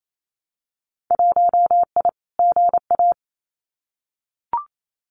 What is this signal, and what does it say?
Morse code, heard as a keyed tone, about 28 wpm, 700 Hz. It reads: 1I GA
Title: Winter Field Day Exchanges Encore 28wpm